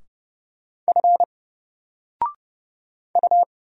Morse code, heard as a keyed tone, about 30 wpm, 700 Hz. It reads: F U
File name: Single-Letter-Number-30wpm